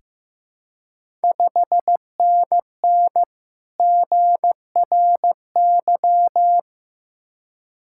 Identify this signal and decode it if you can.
Morse code, heard as a keyed tone, about 15 wpm, 700 Hz. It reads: 5NN GRY
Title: Ontario QSO Party Exchange 15wpm